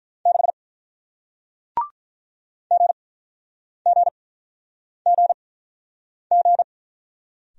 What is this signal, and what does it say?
Morse code, heard as a keyed tone, about 48 wpm, spaced as for 9 wpm, 700 Hz. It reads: 6 GGGG